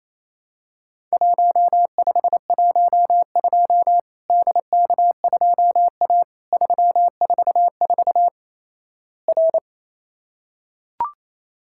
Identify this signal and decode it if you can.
Morse code, heard as a keyed tone, about 28 wpm, 700 Hz. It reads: 1512 DK2A 344 R
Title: Worked All Europe DX Contest - QTC Exchange - 28wpm